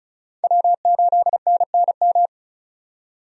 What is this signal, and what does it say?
Morse code, heard as a keyed tone, about 34 wpm, 700 Hz. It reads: W8NNM
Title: USA Call Signs - 5 Characters - 35wpm